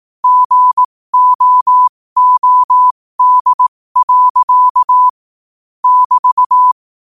Morse code, heard as a keyed tone, about 18 wpm, spaced as for 15 wpm, 1000 Hz. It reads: GOOD. <BT>
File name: qotdfiles / 15wpm